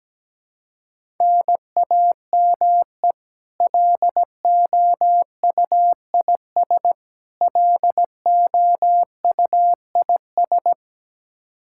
Morse code, heard as a keyed tone, about 17 wpm, 700 Hz. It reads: NAME LOUIS LOUIS
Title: K1USN SST QSO Practice 17wpm